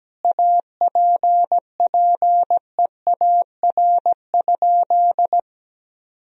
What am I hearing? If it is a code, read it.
Morse code, heard as a keyed tone, about 17 wpm, 700 Hz. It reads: APPEAR?
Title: Sentences from Top 1000 Words - Review Part 2 - 17wpm